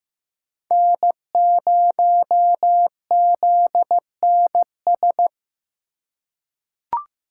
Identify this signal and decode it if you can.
Morse code, heard as a keyed tone, about 15 wpm, 700 Hz. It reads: N0ZNS
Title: USA Call Signs - 5 Characters - 3x Character Spacing - 15wpm